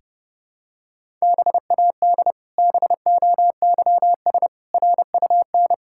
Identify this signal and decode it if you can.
Morse code, heard as a keyed tone, about 30 wpm, 700 Hz. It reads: BAD BOYS RUN
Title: Pangram Challenge v2 - 30wpm